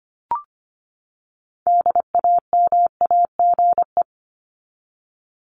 Morse code, heard as a keyed tone, about 25 wpm, 700 Hz. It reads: DAMAGE
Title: Top 1001-1250 Words 25wpm